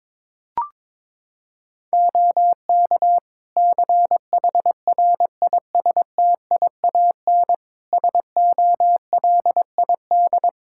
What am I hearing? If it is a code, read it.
Morse code, heard as a keyed tone, about 22 wpm, 700 Hz. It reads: OK CHRISTIAN SOLID
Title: Practice QSO - Exchange 3 - More Info - 22wpm